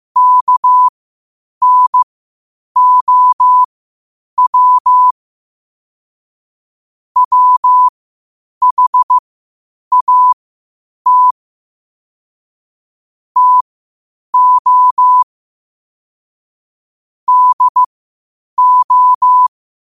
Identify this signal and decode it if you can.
Morse code, heard as a keyed tone, about 15 wpm, spaced as for 8 wpm, 1000 Hz. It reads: KNOW WHAT TO DO
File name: qotdfiles / 8wpm